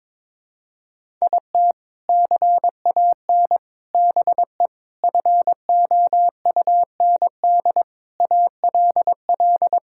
Morse code, heard as a keyed tone, about 22 wpm, 700 Hz. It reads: IT CAN BE FOUND ALL
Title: Sentences from Top 1000 Words - Review Part 5 - 22wpm